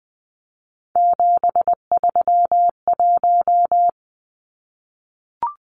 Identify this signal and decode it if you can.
Morse code, heard as a keyed tone, about 20 wpm, 700 Hz. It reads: 731
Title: Numbers - 3-Digit 20wpm